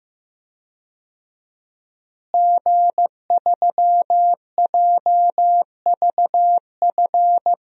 Morse code, heard as a keyed tone, about 15 wpm, 700 Hz. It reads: G3JVF
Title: DXCC-100 Call Signs - 5 Characters - Repeated 2x - 15wpm